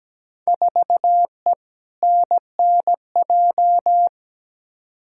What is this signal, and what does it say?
Morse code, heard as a keyed tone, about 17 wpm, 700 Hz. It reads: 4E NNJ
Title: Field Day Exchanges - 17wpm